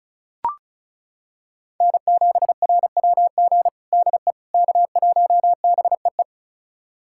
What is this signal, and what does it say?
Morse code, heard as a keyed tone, about 36 wpm, 700 Hz. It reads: N7RWG DE K1BEE
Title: USA Callsign DE Callsign - 5 Characters - 35wpm